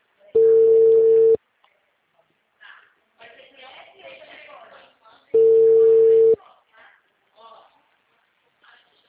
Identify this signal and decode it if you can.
Ringback tone (China, 450 Hz), with speech.
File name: out- -2006- - - .30148